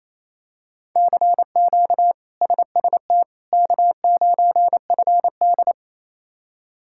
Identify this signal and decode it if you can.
Morse code, heard as a keyed tone, about 27 wpm, 700 Hz. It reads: CQ SST K9FD
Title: K1USN SST QSO Practice - 28wpm